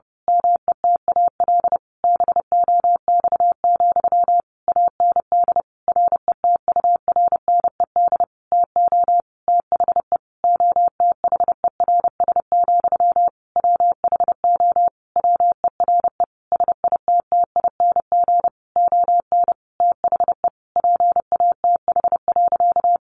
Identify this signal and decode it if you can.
Morse code, heard as a keyed tone, about 30 wpm, 700 Hz. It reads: METAL BOX, AND RETURNED TO THE OTHERS, WHO WERE SITTING ON THE PATH.
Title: A Sound Of Thunder 30wpm